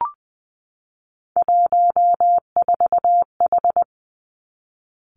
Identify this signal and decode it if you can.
Morse code, heard as a keyed tone, about 19 wpm, 700 Hz. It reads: 14H
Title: Three Challenging Letter-Number Combinations 20wpm